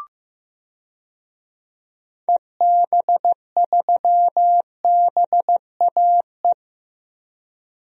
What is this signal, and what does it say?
Morse code, heard as a keyed tone, about 15 wpm, 700 Hz. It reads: EB3BAE
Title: DXCC-100 Call Signs - 6 Characters - Repeated 3x - 15wpm